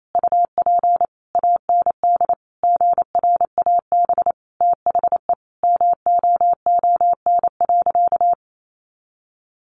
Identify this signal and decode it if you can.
Morse code, heard as a keyed tone, about 28 wpm, 700 Hz. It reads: UP AND GRAB THE MOON.
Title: A Sound Of Thunder 28wpm